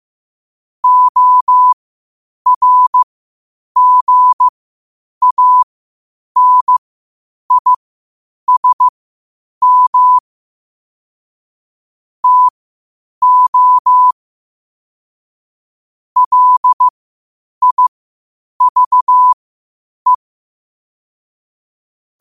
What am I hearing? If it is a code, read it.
Morse code, heard as a keyed tone, about 15 wpm, spaced as for 8 wpm, 1000 Hz. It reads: ORGANISM TO LIVE
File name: qotdfiles / 8wpm